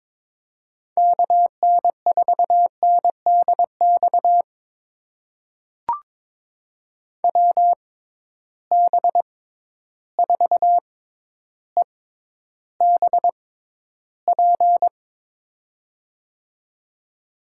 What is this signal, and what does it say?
Morse code, heard as a keyed tone, about 22 wpm, spaced as for 11 wpm, 700 Hz. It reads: KN4NDX WB4EBP
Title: USA Call Signs - 6 Characters - 6x Character Spacing - 22wpm